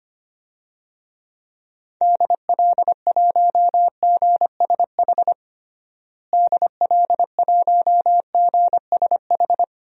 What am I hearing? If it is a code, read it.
Morse code, heard as a keyed tone, about 25 wpm, 700 Hz. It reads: DL1GSH DL1GSH
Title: DXCC-100 Call Signs - 6 Characters - Repeated 2x - 25wpm